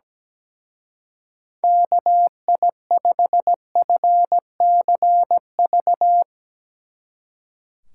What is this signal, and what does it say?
Morse code, heard as a keyed tone, about 19 wpm, 700 Hz. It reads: KI5FCV
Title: Speed Racing - USA 2x3 Call Signs 17wpm